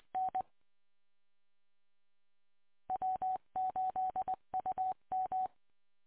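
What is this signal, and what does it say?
Morse code, heard as a keyed tone, about 20 wpm, 750 Hz. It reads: N W8UM
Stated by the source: Morse code recorded from the W8UM repeater on 145.230.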